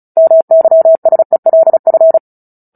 Morse code, heard as a keyed tone, about 34 wpm, 650 Hz. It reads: MYSELF
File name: Book Two / 09-Chapter 9 - 35 WPM